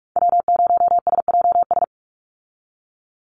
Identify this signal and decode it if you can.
Morse code, heard as a keyed tone, about 44 wpm, 700 Hz. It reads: W0SJS
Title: Call Signs Repeated 2x - 45wpm